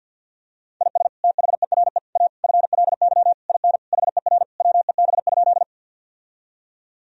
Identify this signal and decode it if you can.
Morse code, heard as a keyed tone, about 50 wpm, 700 Hz. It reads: IS THERE A FLY IN HER WEB?
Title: Sentences from Top 1000 Words - Review Part 4 - 50wpm